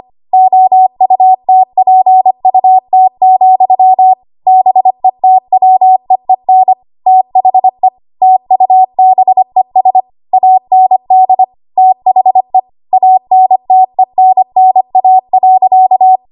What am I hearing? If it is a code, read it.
Morse code, heard as a keyed tone, about 25 wpm, 750 Hz. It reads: OUTPUT, BETWEEN THE TUBES AND THE ANTENNA.
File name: Archive / 25 WPM / 25WPM